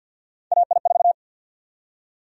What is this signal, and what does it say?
Morse code, heard as a keyed tone, about 51 wpm, 700 Hz. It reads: AI4
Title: USA Call Sign Prefixes - Repeated 2x - 50wpm